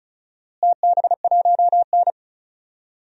Morse code, heard as a keyed tone, about 36 wpm, 700 Hz. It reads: TB1N
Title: DXCC-100 Call Signs - 4 Characters - Repeated 2x - 35wpm